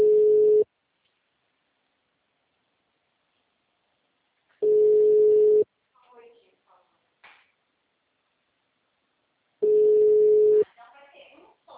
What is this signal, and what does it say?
Ringback tone (Europe, 425 Hz), with speech.